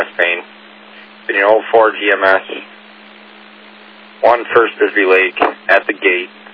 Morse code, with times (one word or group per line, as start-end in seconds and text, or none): none